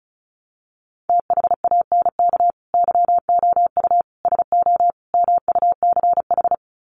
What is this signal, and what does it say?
Morse code, heard as a keyed tone, about 34 wpm, 700 Hz. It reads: THANK YOU SO MUCH
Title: Sets of 4 Words 35wpm